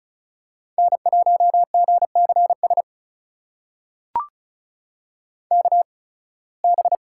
Morse code, heard as a keyed tone, about 36 wpm, spaced as for 23 wpm, 700 Hz. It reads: N1GCS KB
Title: USA Call Signs - 8x Character Spacing - 35wpm